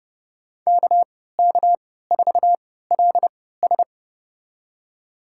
Morse code, heard as a keyed tone, about 30 wpm, spaced as for 17 wpm, 700 Hz. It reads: KK4LS
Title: USA Call Signs - 3x Character Spacing - 30wpm